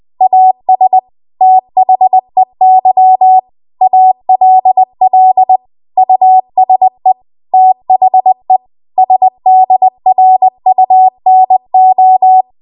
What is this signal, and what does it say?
Morse code, heard as a keyed tone, about 20 wpm, 750 Hz. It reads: AS THEY ALL USE THE SDRUNO
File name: Archive / 20 WPM / 20WPM